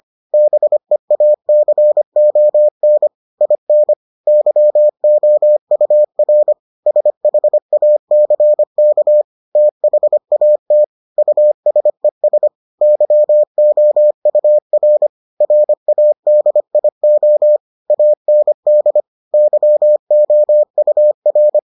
Morse code, heard as a keyed tone, about 25 wpm, 600 Hz. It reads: BEACON IN YOUR SHACK THAT USES YOUR RADIO AND YOUR